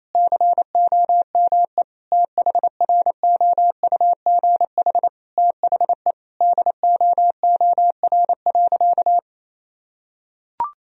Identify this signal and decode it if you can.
Morse code, heard as a keyed tone, about 28 wpm, 700 Hz. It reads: COME THROUGH THE DOOR.
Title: Sentences from Top 1000 Words - Review Part 1 - 28wpm